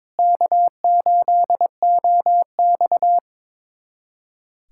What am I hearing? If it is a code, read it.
Morse code, heard as a keyed tone, about 22 wpm, 700 Hz. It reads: K8OX
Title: USA Callsign DE Callsign - 4 Characters - 22wpm